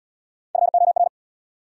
Morse code, heard as a keyed tone, about 79 wpm, 700 Hz. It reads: FOR